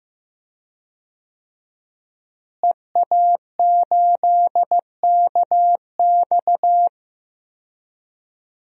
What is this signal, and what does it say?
Morse code, heard as a keyed tone, about 15 wpm, 700 Hz. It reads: EA8KX
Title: DXCC-100 Call Signs - 5 Characters - Repeated 3x - 15wpm